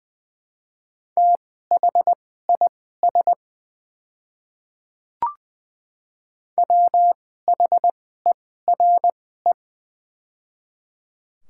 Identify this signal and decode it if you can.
Morse code, heard as a keyed tone, about 20 wpm, spaced as for 14 wpm, 700 Hz. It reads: THIS WHERE
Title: 2x Character Spacing - Top 300 Words - 20wpm